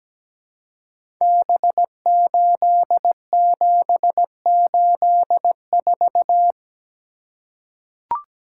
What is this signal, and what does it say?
Morse code, heard as a keyed tone, about 17 wpm, 700 Hz. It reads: B8784
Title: Code Groups - 5 Letter:Numbers - 17wpm